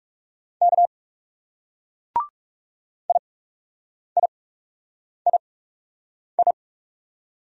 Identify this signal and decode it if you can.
Morse code, heard as a keyed tone, about 37 wpm, spaced as for 8 wpm, 700 Hz. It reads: K IIII
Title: Speed-Racing Single Letters 30wpm